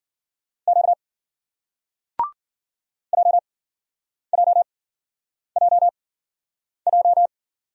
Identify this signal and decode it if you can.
Morse code, heard as a keyed tone, about 54 wpm, spaced as for 9 wpm, 700 Hz. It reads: Q JJJJ